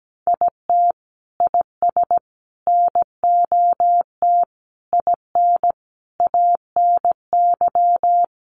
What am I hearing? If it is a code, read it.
Morse code, heard as a keyed tone, about 17 wpm, 700 Hz. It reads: IT IS NOT IN ANY
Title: Sentences from Top 1200 Words 17wpm